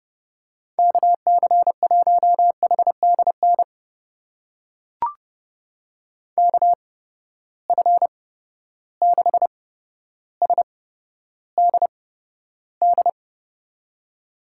Morse code, heard as a keyed tone, about 30 wpm, spaced as for 13 wpm, 700 Hz. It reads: KC1HDN KF6SDD
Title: USA Call Signs - 6 Characters - 8x Character Spacing - 30wpm